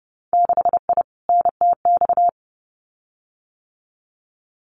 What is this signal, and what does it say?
Morse code, heard as a keyed tone, about 30 wpm, 700 Hz. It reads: 6I NTX
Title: Winter Field Day Exchanges 30wpm